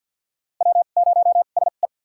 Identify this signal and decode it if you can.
Morse code, heard as a keyed tone, about 49 wpm, spaced as for 36 wpm, 700 Hz. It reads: W0SE
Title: USA Call Signs - 4 Characters - 2x Character Spacing - 50wpm